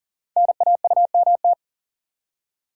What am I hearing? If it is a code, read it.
Morse code, heard as a keyed tone, about 40 wpm, 700 Hz. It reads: NAUMT
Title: Code Groups - 5 Letters - 40wpm